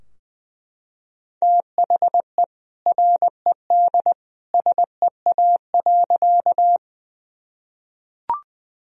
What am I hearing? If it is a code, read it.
Morse code, heard as a keyed tone, about 20 wpm, 700 Hz. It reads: THE RED SEA.
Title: Sentences from Top 1000 Words - Review Part 2 - 20wpm